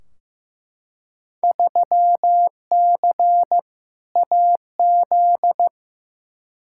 Morse code, heard as a keyed tone, about 15 wpm, 700 Hz. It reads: 3C AZ
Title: Field Day Exchanges Encore - 15wpm